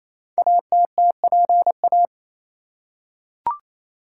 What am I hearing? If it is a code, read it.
Morse code, heard as a keyed tone, about 29 wpm, 700 Hz. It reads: ATTPA
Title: Code Groups - 5 Letters - 28wpm